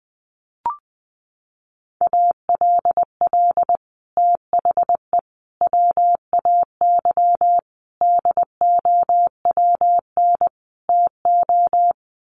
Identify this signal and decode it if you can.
Morse code, heard as a keyed tone, about 20 wpm, 700 Hz. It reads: ALL THE WAY DOWN TO
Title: Sets of 5 Words Encore2 20wpm